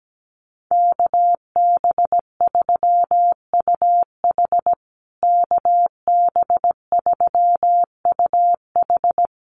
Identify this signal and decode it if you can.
Morse code, heard as a keyed tone, about 17 wpm, 700 Hz. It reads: KB3UH KB3UH
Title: USA Call Signs Repeated 3x 17wpm